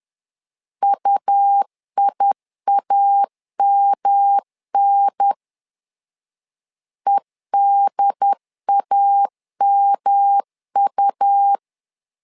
Morse code, heard as a keyed tone, about 11 wpm, 800 Hz. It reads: UIAMN EDAMU
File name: LEZIONI / 3 MWRO / 3 M 10wpm 1828